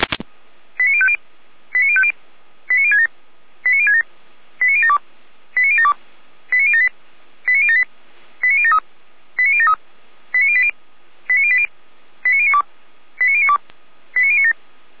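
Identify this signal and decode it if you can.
Five-tone selective call (ZVEI), heard as quick runs of five tones, tones of about 70 ms. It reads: 89050 89050 89067 89067 89072 89072 89078 89078 89073 89073 89080 89080 89092 89092 89097
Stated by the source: ZVEI five-tone selective call 89022 received with an SDR, fire brigade Feuerwehr Orlamünde.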